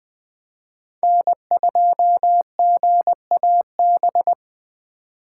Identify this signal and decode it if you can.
Morse code, heard as a keyed tone, about 23 wpm, 700 Hz. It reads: N2GAB
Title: Speed Racing - USA Call Signs 20wpm